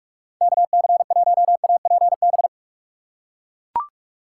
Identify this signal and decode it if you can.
Morse code, heard as a keyed tone, about 41 wpm, 700 Hz. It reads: KC1APB
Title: USA Call Signs - 2x Character Spacing - 45wpm